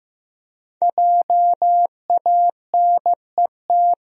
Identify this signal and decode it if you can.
Morse code, heard as a keyed tone, about 15 wpm, 700 Hz. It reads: JANET
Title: USA Common Names - 15wpm